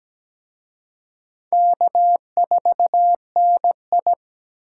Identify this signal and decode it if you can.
Morse code, heard as a keyed tone, about 20 wpm, 700 Hz. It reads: K4NI
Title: Speed Racing - USA 1x2 Call Signs 17wpm